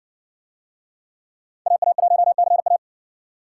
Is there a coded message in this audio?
Yes: Morse code from 1.7 to 2.8 s, heard as a keyed tone, about 60 wpm, 700 Hz. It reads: AA1CA